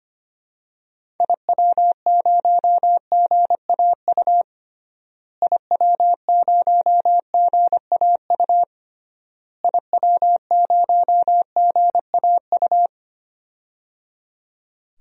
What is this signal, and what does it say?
Morse code, heard as a keyed tone, about 25 wpm, 700 Hz. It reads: IW0GAU IW0GAU IW0GAU
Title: DXCC-100 Call Signs - 6 Characters - Repeated 3x - 25wpm